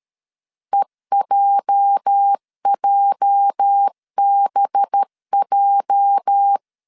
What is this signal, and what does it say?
Morse code, heard as a keyed tone, about 13 wpm, 800 Hz. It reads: EJJBJ
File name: LEZIONI / 8 YJ45 / 8 YJ 12wpm 2235